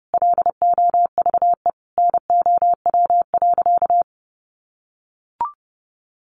Morse code, heard as a keyed tone, about 30 wpm, 700 Hz. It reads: LOVE NOW.
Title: Sentences from Top 1300 Words 30wpm